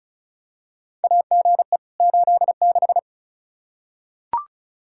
Morse code, heard as a keyed tone, about 36 wpm, 700 Hz. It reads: AGE 86
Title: K1USN SST QSO Practice 35wpm